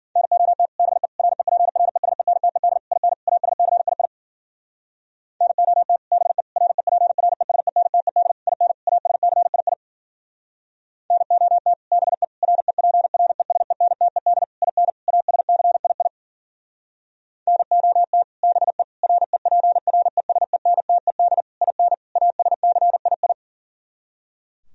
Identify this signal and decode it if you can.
Morse code, heard as a keyed tone, about 51 wpm, 700 Hz. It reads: NOT BE REPRESENTED IN ASCII NOT BE REPRESENTED IN ASCII NOT BE REPRESENTED IN ASCII NOT BE REPRESENTED IN ASCII